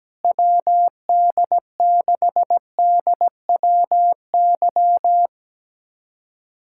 Morse code, heard as a keyed tone, about 17 wpm, 700 Hz. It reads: WD6DWY
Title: USA Callsign DE Callsign - 6 Characters - 17wpm